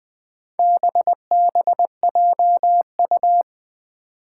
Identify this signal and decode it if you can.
Morse code, heard as a keyed tone, about 20 wpm, 700 Hz. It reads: BBJU
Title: Four Challenging Letter-Number Combinations 20wpm